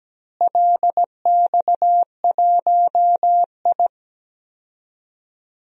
Morse code, heard as a keyed tone, about 17 wpm, 700 Hz. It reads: LX1I
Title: DXCC-100 Call Signs - 4 Characters - Repeated 2x - 17wpm